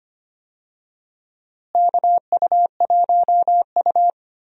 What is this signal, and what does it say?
Morse code, heard as a keyed tone, about 25 wpm, 700 Hz. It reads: KU1U
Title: USA Call Signs - 4 Characters - Repeated 2x - 25wpm